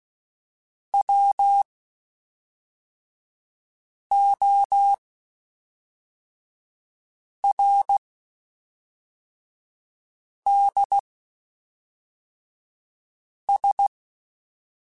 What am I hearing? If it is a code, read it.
Morse code, heard as a keyed tone, about 16 wpm, spaced as for 3 wpm, 800 Hz. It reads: WORDS
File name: Spaced / Chuckles Spaced-mp3s / SoftWords sp @5